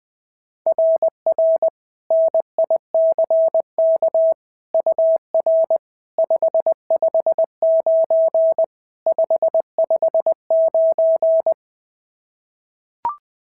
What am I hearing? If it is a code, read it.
Morse code, heard as a keyed tone, about 20 wpm, 690 Hz. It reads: RR NICK UR 559 559
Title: SOTA QSO Practice 20wpm